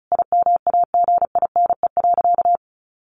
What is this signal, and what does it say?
Morse code, heard as a keyed tone, about 34 wpm, 700 Hz. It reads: IMAGINE.
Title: Sentences from Top 1500 Words 35wpm